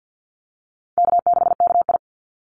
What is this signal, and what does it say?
Morse code, heard as a keyed tone, about 48 wpm, 700 Hz. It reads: K6KI
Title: Easy Call Signs Repeated 2x - 50wpm